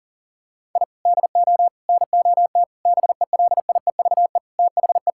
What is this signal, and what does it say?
Morse code, heard as a keyed tone, about 40 wpm, 700 Hz. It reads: I DO NOT BELIEVE THE
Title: Sentences from Top 1000 Words - Review Part 6 - 40wpm